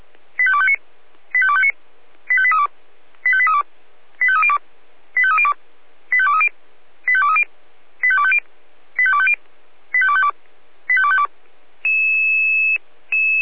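Five-tone selective call (ZVEI), heard as quick runs of five tones, tones of about 70 ms. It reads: 85259 85259 85832 85832 85302 85302 85230 85230 85250 85250 85252 85252
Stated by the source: ZVEI five-tone selective call received on an SDR.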